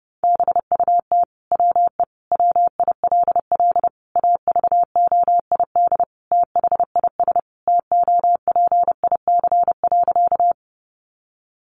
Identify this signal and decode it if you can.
Morse code, heard as a keyed tone, about 30 wpm, 700 Hz. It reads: BUT WE WILL AVOID THIS TOPIC.
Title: Sentences from Top 1200 Words 30wpm